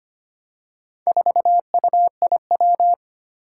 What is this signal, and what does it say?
Morse code, heard as a keyed tone, about 25 wpm, 700 Hz. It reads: 4UIW
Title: Four Challenging Letter-Number Combinations 25wpm